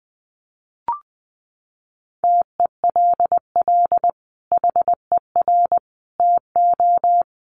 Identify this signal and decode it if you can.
Morse code, heard as a keyed tone, about 20 wpm, 700 Hz. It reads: TELL HER TO
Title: Sentences from Top 300 Words 20wpm